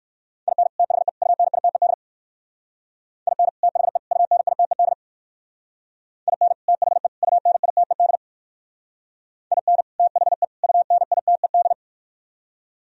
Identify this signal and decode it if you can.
Morse code, heard as a keyed tone, about 56 wpm, 700 Hz. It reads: IN THE UNITED IN THE UNITED IN THE UNITED IN THE UNITED
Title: Speed-Racing - Sets of 3 Words 45wpm